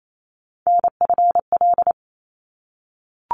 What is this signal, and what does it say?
Morse code, heard as a keyed tone, about 28 wpm, 700 Hz. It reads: NFL
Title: ARRL Sections 28wpm